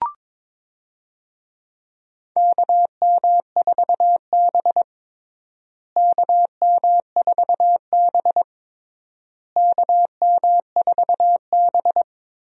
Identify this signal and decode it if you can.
Morse code, heard as a keyed tone, about 22 wpm, 700 Hz. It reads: KM4B KM4B KM4B
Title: USA Call Signs - 4 Characters - Repeated 3x - 22wpm